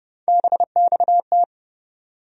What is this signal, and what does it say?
Morse code, heard as a keyed tone, about 32 wpm, 700 Hz. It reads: BXT
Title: USA Call Sign Postfixes - 3 Letters 30wpm